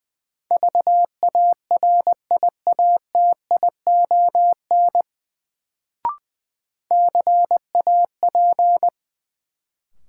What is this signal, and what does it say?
Morse code, heard as a keyed tone, about 21 wpm, 700 Hz. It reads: VARIATION CAP
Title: Sentences from Top 1600 Words - New Word Drill - 20wpm